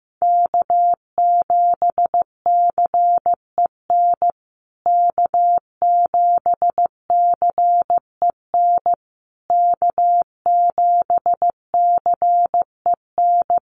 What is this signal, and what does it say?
Morse code, heard as a keyed tone, about 15 wpm, 700 Hz. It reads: K7CEN K7CEN K7CEN
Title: USA Call Signs Repeated 3x 15wpm